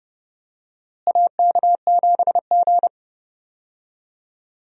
Morse code, heard as a keyed tone, about 30 wpm, 700 Hz. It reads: AK7G
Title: USA Call Signs - 4 Characters - Repeated 2x - 30wpm